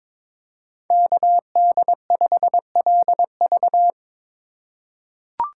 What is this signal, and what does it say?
Morse code, heard as a keyed tone, about 22 wpm, 700 Hz. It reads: KD5LV
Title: USA Call Signs - 5 Characters - 3x Character Spacing - 22wpm